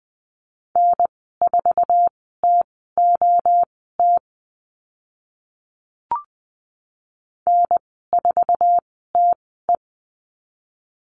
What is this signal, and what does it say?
Morse code, heard as a keyed tone, about 20 wpm, spaced as for 14 wpm, 700 Hz. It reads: N4TOT N4TE
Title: 062 Lesson 11g - Letter h - Call Signs 20wpm